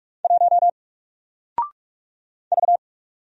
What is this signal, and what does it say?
Morse code, heard as a keyed tone, about 44 wpm, 700 Hz. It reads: 1 V
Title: Single Letter-Number 45wpm